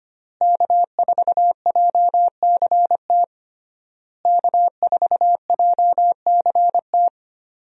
Morse code, heard as a keyed tone, about 25 wpm, 700 Hz. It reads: K4JCT K4JCT
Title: USA Call Signs - 5 Characters - Repeated 3x - 25wpm